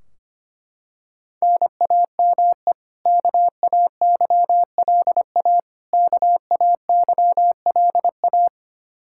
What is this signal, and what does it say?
Morse code, heard as a keyed tone, about 25 wpm, 700 Hz. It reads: NAME KAYLA KAYLA
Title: Practice QSO - Exchange 1 - Beginning - 25wpm